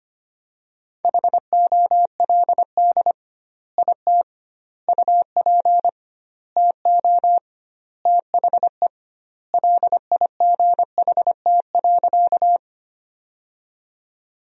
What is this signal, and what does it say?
Morse code, heard as a keyed tone, about 25 wpm, 700 Hz. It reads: HOLD IT UP TO THE LIGHT.
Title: 2x Word-Spacing - Sentences from Top 400 Words - 25wpm